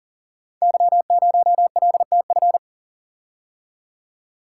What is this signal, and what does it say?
Morse code, heard as a keyed tone, about 39 wpm, 700 Hz. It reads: Y0LTF
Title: Code Groups - 5 Letter:Numbers - 40wpm